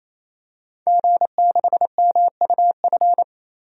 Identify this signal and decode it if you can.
Morse code, heard as a keyed tone, about 28 wpm, 700 Hz. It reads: G6MUF
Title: DXCC-100 Call Signs - 5 Characters - 28wpm